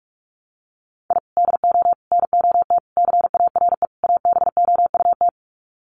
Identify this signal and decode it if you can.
Morse code, heard as a keyed tone, about 44 wpm, 700 Hz. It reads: I DO NOT CARE ABOUT
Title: Sets of 5 Words Encore2 45wpm